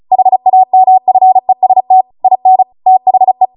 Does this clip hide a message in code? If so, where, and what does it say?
Yes: Morse code from 0.1 to 3.5 s, heard as a keyed tone, about 36 wpm, 750 Hz. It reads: HAMFEST IN THE